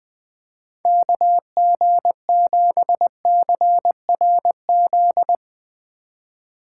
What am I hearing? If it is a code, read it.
Morse code, heard as a keyed tone, about 20 wpm, 700 Hz. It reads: KG7CRZ
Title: USA Call Signs - 6 Characters - 2x Character Spacing - 20wpm